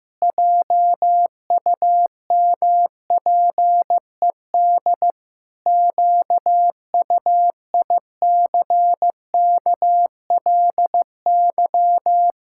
Morse code, heard as a keyed tone, about 15 wpm, 700 Hz. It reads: JUMPED QUICKLY
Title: Pangram Challenge v2 - 15wpm